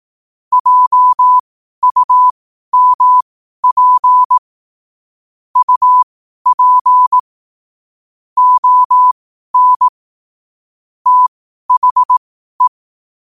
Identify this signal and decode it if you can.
Morse code, heard as a keyed tone, about 18 wpm, spaced as for 12 wpm, 1000 Hz. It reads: JUMP UP ON THE
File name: qotdfiles / 12wpm